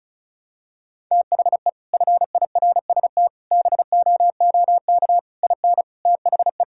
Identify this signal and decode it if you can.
Morse code, heard as a keyed tone, about 34 wpm, 700 Hz. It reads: THE FIRST BOOK IN THE
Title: Sentences from Top 1000 Words - Review Part 5 - 35wpm